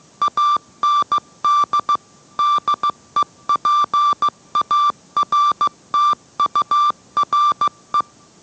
Morse code, heard as a keyed tone, about 15 wpm, 1180 Hz. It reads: AND DEPARTURE